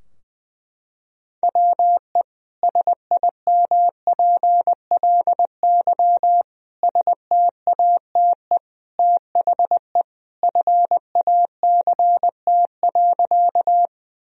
Morse code, heard as a keyed tone, about 20 wpm, 700 Hz. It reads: WE SIMPLY STATE THE FACT.
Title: Sentences from Top 1000 Words - Review Part 4 - 20wpm